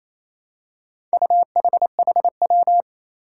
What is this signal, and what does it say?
Morse code, heard as a keyed tone, about 28 wpm, 700 Hz. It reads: UHHW
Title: Four Challenging Letter-Number Combinations 28wpm